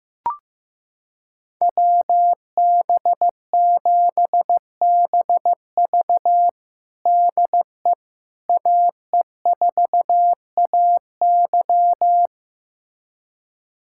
Morse code, heard as a keyed tone, about 16 wpm, 700 Hz. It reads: WB7BV DE AE4AY
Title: USA Callsign DE Callsign - 5 Characters - 15wpm